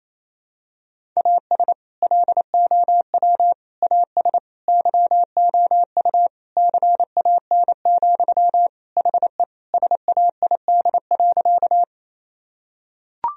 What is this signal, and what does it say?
Morse code, heard as a keyed tone, about 28 wpm, 700 Hz. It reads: AS LOW AS YOU CAN, HE SAID.
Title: Sentences from Top 1000 Words - Review Part 1 - 28wpm